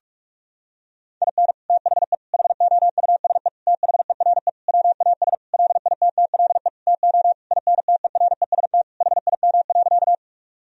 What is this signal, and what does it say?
Morse code, heard as a keyed tone, about 46 wpm, 700 Hz. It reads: IN THE HOUSE THERE WAS LITTLE TO INTEREST HIM.
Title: Sentences from Top 1000 Words - Review Part 2 - 45wpm